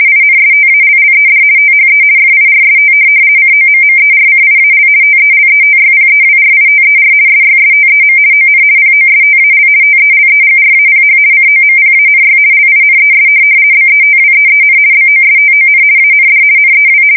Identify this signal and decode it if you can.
RTTY, about 50 baud. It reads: INSTALLED THE FIRST COMMERCIAL TELETYPEWRITER SYSTEM ON POSTAL TELEGRAPH COMPANY LINES BETWEEN BOSTON AND NEW
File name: fsk170-50B / teletype-corp